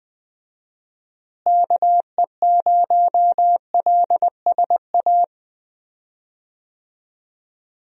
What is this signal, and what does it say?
Morse code, heard as a keyed tone, about 20 wpm, 700 Hz. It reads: KE0LSA
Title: USA Call Signs - 6 Characters - Repeated 2x - 20wpm